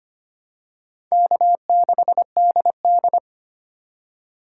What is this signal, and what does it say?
Morse code, heard as a keyed tone, about 25 wpm, 700 Hz. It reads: K6DD
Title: USA Call Signs - 4 Characters - 4x Character Spacing - 25wpm